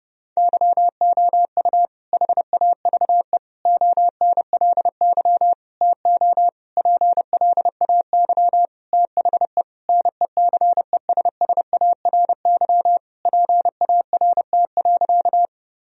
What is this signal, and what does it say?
Morse code, heard as a keyed tone, about 32 wpm, 700 Hz. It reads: YOU HAVE ONLY TO PLAY THE NECESSARY PART.
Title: Sentences from Top 1000 Words - Review Part 1 - 30wpm